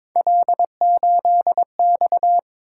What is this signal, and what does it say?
Morse code, heard as a keyed tone, about 22 wpm, 700 Hz. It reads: L8X
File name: practice / 22wpm / Three Letter-Number Combinations - 22wpm